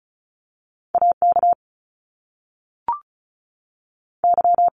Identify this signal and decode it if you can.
Morse code, heard as a keyed tone, about 36 wpm, 700 Hz. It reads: AK Y
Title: Top Two Letter Combinations in Words 35wpm